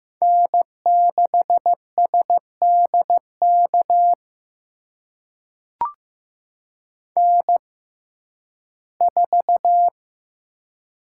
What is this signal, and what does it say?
Morse code, heard as a keyed tone, about 15 wpm, spaced as for 11 wpm, 700 Hz. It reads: N6SDK N4
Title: USA Call Signs - 5 Characters - 6x Character Spacing - 15wpm